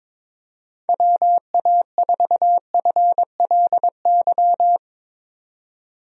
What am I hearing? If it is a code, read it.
Morse code, heard as a keyed tone, about 21 wpm, 700 Hz. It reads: WA4FLY
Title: USA Call Signs - 6 Characters - 22wpm